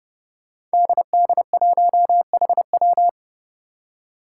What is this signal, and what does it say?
Morse code, heard as a keyed tone, about 29 wpm, 700 Hz. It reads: DD1HW
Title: Five Challenging Letter-Number Combinations 30wpm